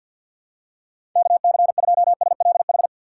Morse code, heard as a keyed tone, about 48 wpm, 700 Hz. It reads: KC2SLH
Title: USA Call Signs - 6 Characters - 3x Character Spacing - 50wpm